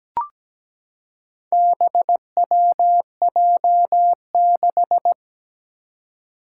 Morse code, heard as a keyed tone, about 17 wpm, 700 Hz. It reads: BWJ6
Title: Four Challenging Letter-Number Combinations 17wpm